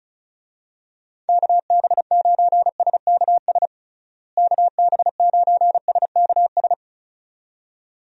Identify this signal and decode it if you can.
Morse code, heard as a keyed tone, about 36 wpm, 700 Hz. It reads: KB9SKS KB9SKS
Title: USA Call Signs - 6 Characters - Repeated 2x - 35wpm